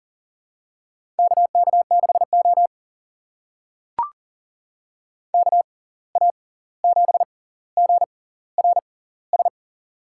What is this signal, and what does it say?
Morse code, heard as a keyed tone, about 38 wpm, spaced as for 18 wpm, 700 Hz. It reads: KK6O KA7GRS
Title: USA Call Signs - 6x Character Spacing - 40wpm